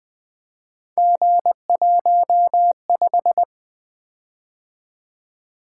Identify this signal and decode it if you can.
Morse code, heard as a keyed tone, about 20 wpm, 700 Hz. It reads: G15
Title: Three Letter-Number Combinations - 20wpm